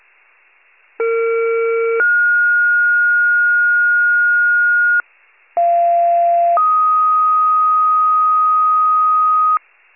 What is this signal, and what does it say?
Two-tone page, heard as a one-second tone, then a three-second tone, about 460 then 1430 Hz, about 690 then 1250 Hz, with noise. It is heard from 1.0 to 9.6 s.